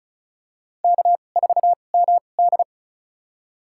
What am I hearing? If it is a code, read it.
Morse code, heard as a keyed tone, about 34 wpm, spaced as for 25 wpm, 700 Hz. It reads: K4MD
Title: USA Call Signs - 2x Character Spacing - 35wpm